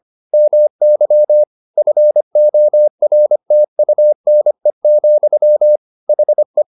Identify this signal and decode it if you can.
Morse code, heard as a keyed tone, about 25 wpm, 600 Hz. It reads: MY FORTUNE, HE